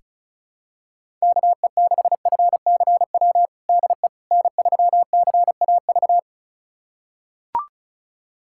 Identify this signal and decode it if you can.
Morse code, heard as a keyed tone, about 36 wpm, 700 Hz. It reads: KE6FCW DE N3CAV
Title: USA Callsign DE Callsign 35wpm